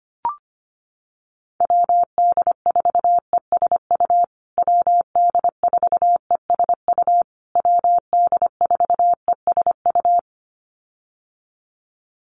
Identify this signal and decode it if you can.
Morse code, heard as a keyed tone, about 25 wpm, 700 Hz. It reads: WD4ESU WD4ESU WD4ESU
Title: USA Call Signs Repeated 3x 25wpm